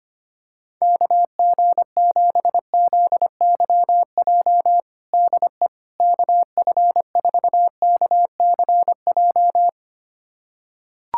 Morse code, heard as a keyed tone, about 25 wpm, 700 Hz. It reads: KG7ZYJ DE KF4KCJ
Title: USA Callsign DE Callsign - 6 Characters - 25wpm